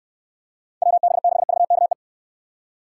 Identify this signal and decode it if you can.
Morse code, heard as a keyed tone, about 66 wpm, 700 Hz. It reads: WB6VKE